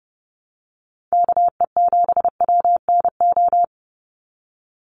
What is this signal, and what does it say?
Morse code, heard as a keyed tone, about 30 wpm, 700 Hz. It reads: KE7WNO
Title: USA Call Signs 30wpm